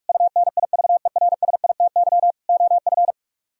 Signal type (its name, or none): Morse code